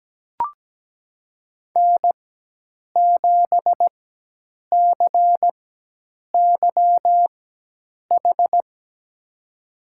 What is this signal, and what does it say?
Morse code, heard as a keyed tone, about 17 wpm, spaced as for 8 wpm, 700 Hz. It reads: N7CYH
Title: USA Call Signs - 5 Characters - 4x Character Spacing - 17wpm